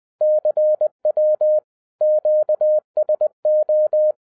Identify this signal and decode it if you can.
Morse code, heard as a keyed tone, about 20 wpm, 610 Hz. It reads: CW QSO
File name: 2012-01-mp3 / n05-CAME BACK TO HAM RADIO AFTER 30 YEARS OF INTERVAL-20wpm